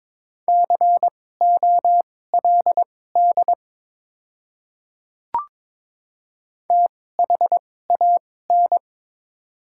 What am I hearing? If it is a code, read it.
Morse code, heard as a keyed tone, about 22 wpm, spaced as for 16 wpm, 690 Hz. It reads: COLD THAN
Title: 2x Character Spacing - Top 500 Words - 22wpm